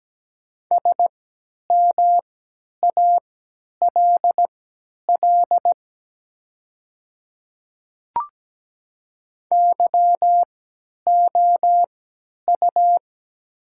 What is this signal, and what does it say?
Morse code, heard as a keyed tone, about 17 wpm, spaced as for 10 wpm, 700 Hz. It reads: SMALL YOU
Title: 3x Character Spacing - Top 200 Words - 17wpm